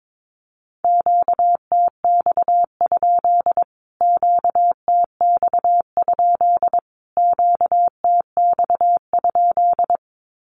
Morse code, heard as a keyed tone, about 22 wpm, 700 Hz. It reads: QTX? QTX? QTX?
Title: Q-Codes Repeated 3x 22wpm